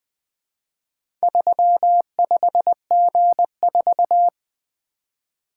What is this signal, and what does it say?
Morse code, heard as a keyed tone, about 24 wpm, 700 Hz. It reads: 35G4
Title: Speed-Racing - Four Letter-Number Combinations - 20wpm